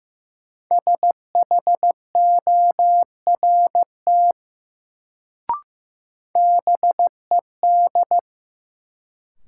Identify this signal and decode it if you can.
Morse code, heard as a keyed tone, about 16 wpm, 700 Hz. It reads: SHORT BED
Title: Sets of 5 Words Encore - New Word Drill - 15wpm